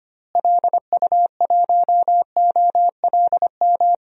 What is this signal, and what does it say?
Morse code, heard as a keyed tone, about 25 wpm, 700 Hz. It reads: LU1OLM
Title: DXCC-100 Call Signs - 6 Characters - Repeated 3x - 25wpm